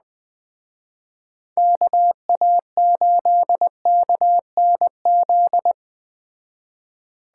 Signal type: Morse code

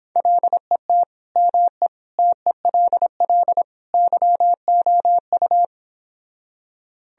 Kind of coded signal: Morse code

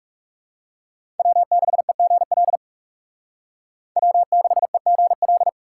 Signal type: Morse code

